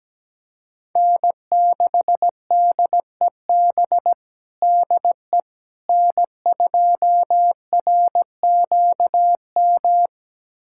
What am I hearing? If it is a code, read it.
Morse code, heard as a keyed tone, about 17 wpm, 700 Hz. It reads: N6DEB DE N2RQM
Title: USA Callsign DE Callsign - 5 Characters - 17wpm